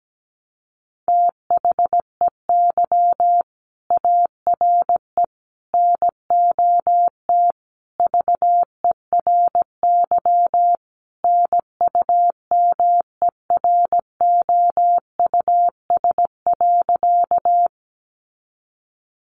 Morse code, heard as a keyed tone, about 17 wpm, 700 Hz. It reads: THEY ARE NOT VERY NUMEROUS.
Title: Sentences from Top 1400 Words 17wpm